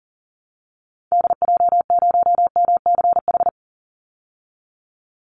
Morse code, heard as a keyed tone, about 40 wpm, 700 Hz. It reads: DJ0MCH
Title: Call Signs Repeated 3x - 40wpm